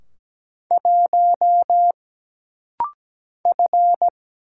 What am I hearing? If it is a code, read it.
Morse code, heard as a keyed tone, about 17 wpm, 700 Hz. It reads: 1 F
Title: Single Letter-Number Alt - Rapid-Fire 17wpm